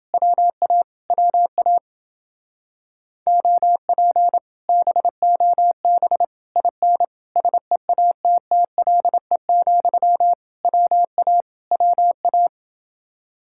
Morse code, heard as a keyed tone, about 29 wpm, 700 Hz. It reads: WA WA OP BOB IN SEATTLE, WA WA